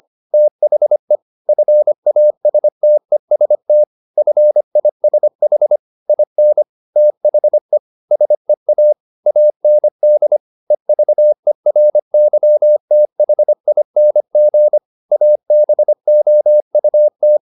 Morse code, heard as a keyed tone, about 25 wpm, 600 Hz. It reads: THE FASTEST FISH IN THE SEA AND EVERYTHING ABOUT